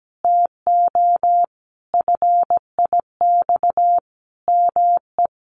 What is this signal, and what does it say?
Morse code, heard as a keyed tone, about 17 wpm, 700 Hz. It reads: TO FIX ME
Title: Sentences from Top 1400 Words 17wpm